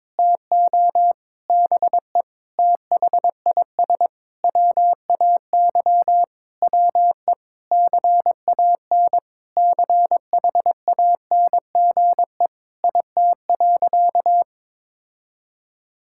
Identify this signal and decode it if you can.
Morse code, heard as a keyed tone, about 22 wpm, 700 Hz. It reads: TO BE THIS WAY WE CAN CHANGE IT.
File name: Sentences with 7-Word Phrases - 22wpm